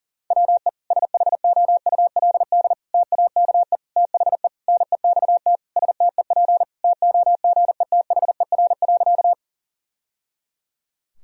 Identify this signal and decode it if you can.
Morse code, heard as a keyed tone, about 40 wpm, 700 Hz. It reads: WE SHOULD TAKE THE NEXT STEP TOGETHER.
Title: Sentences from Top 1000 Words - Review Part 2 - 40wpm